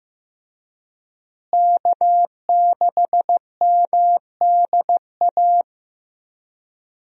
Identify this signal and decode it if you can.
Morse code, heard as a keyed tone, about 15 wpm, 700 Hz. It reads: K6MDA
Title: USA Call Signs - 5 Characters - Repeated 3x - 15wpm